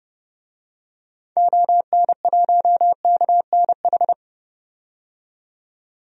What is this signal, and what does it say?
Morse code, heard as a keyed tone, about 30 wpm, 700 Hz. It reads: ON1KNH
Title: DXCC-100 Call Signs - 6 Characters - 30wpm